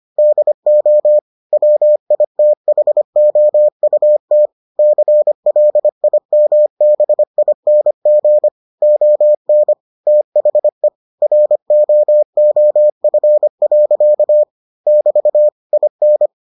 Morse code, heard as a keyed tone, about 25 wpm, 600 Hz. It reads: DO WITHOUT CLIMBING ON THE ROOF. <BT> IN